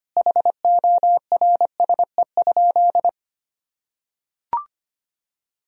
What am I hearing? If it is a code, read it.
Morse code, heard as a keyed tone, about 25 wpm, 700 Hz. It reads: HORSE?
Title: Sentences from Top 500 Words and Abbreviations 25wpm